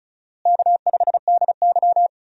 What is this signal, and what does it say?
Morse code, heard as a keyed tone, about 36 wpm, 700 Hz. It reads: K5DY
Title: USA Call Signs - 4 Characters - Repeated 3x - 35wpm